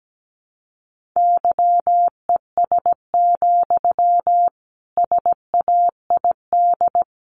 Morse code, heard as a keyed tone, about 17 wpm, 700 Hz. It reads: YES, SAID
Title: Sentences from Top 500 Words 17wpm